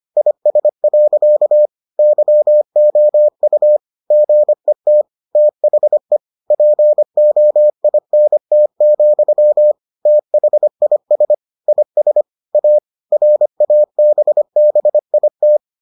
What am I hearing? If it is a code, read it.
Morse code, heard as a keyed tone, about 25 wpm, 600 Hz. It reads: IS. YOU GET THE POINT, THIS IS A RABBIT